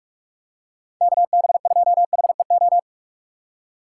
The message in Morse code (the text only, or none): KB2HEO